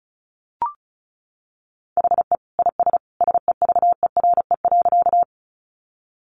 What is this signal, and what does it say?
Morse code, heard as a keyed tone, about 34 wpm, 700 Hz. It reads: HE IS SEVERE.
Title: Sentences from Top 1400 Words 35wpm